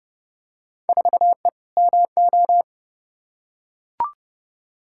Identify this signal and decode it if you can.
Morse code, heard as a keyed tone, about 31 wpm, 700 Hz. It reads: 4E MO
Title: Field Day Exchanges Encore - 30wpm